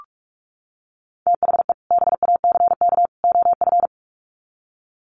Morse code, heard as a keyed tone, about 44 wpm, 700 Hz. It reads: THE BACK OF
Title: Sets of 3 Words 45wpm